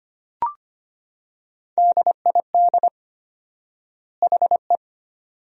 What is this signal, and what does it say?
Morse code, heard as a keyed tone, about 25 wpm, 700 Hz. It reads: DID HE
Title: 4x Word-Spacing - Sentences from Top 400 Words - 25wpm